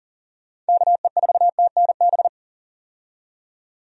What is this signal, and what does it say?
Morse code, heard as a keyed tone, about 40 wpm, 700 Hz. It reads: KE4TNB
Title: USA Call Signs - 6 Characters - 40wpm